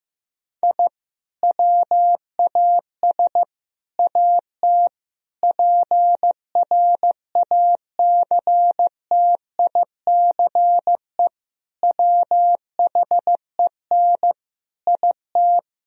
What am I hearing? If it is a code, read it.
Morse code, heard as a keyed tone, about 15 wpm, 700 Hz. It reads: I WAS AT PRACTICE WHEN IT
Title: Sentences from Top 1000 Words - Review Part 4 - 15wpm